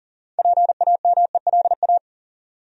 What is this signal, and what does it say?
Morse code, heard as a keyed tone, about 40 wpm, 700 Hz. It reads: PAMELA
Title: USA Common Names - 40wpm